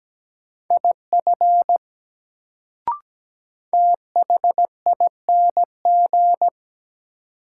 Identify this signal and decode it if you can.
Morse code, heard as a keyed tone, about 17 wpm, 700 Hz. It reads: IF THING
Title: Sentences from Top 100 Words - New Word Drill - 17wpm